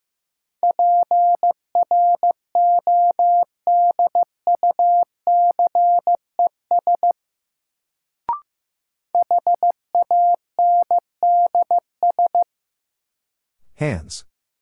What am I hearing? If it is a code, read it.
Morse code, heard as a keyed tone, about 15 wpm, 700 Hz. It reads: PRODUCES HANDS
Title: Sentences from Top 1000 Words - Review Part 6 - New Word Drill - 15wpm